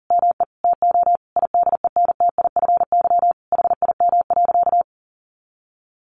Morse code, heard as a keyed tone, about 40 wpm, 700 Hz. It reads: ME TO IDENTIFY HIM.
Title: Sentences from Top 900 Words 40wpm